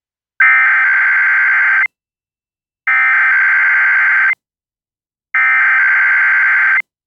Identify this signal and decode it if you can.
Emergency Alert System (SAME) tones (header bursts), heard as harsh buzzing bursts. It is heard from 0.4 to 6.8 s.